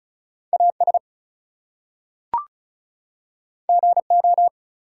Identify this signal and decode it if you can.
Morse code, heard as a keyed tone, about 36 wpm, 700 Hz. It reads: AS GO
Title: Two Letter Words - 35wpm